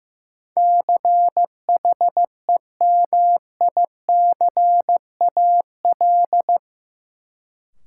Morse code, heard as a keyed tone, about 15 wpm, 700 Hz. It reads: CHEMICAL
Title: Sentences from Top 1200 Words - New Word Drill - 15wpm